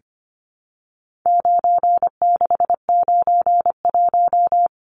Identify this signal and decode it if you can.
Morse code, heard as a keyed tone, about 25 wpm, 700 Hz. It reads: 9691
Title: Numbers - 4-Digit 25wpm